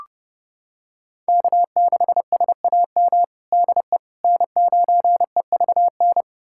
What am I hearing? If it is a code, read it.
Morse code, heard as a keyed tone, about 30 wpm, 700 Hz. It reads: K6SAM DE N9EVN
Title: USA Callsign DE Callsign - 5 Characters - 30wpm